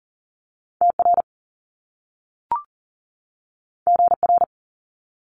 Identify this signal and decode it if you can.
Morse code, heard as a keyed tone, about 39 wpm, 700 Hz. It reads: TR GR
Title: Top Two Letter Combinations in Words 40wpm